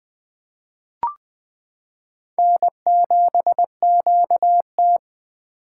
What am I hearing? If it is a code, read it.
Morse code, heard as a keyed tone, about 20 wpm, 700 Hz. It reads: N7QT
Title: K1USN SST QSO Practice - 20wpm